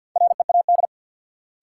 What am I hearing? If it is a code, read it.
Morse code, heard as a keyed tone, about 52 wpm, 700 Hz. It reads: READ
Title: Sentences from Top 200 Words - New Word Drill - 50wpm